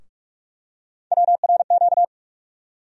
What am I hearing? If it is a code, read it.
Morse code, heard as a keyed tone, about 46 wpm, 700 Hz. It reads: WRQ
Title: USA Call Sign Postfixes - 3 Letters 45wpm